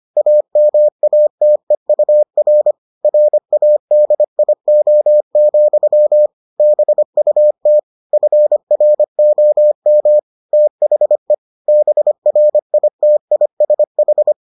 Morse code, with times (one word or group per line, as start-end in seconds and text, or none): 0.2-2.7 AMATEUR
3.0-6.3 RADIO,
6.6-7.8 BUT
8.1-10.2 FROM
10.5-11.3 THE
11.7-14.3 BRITISH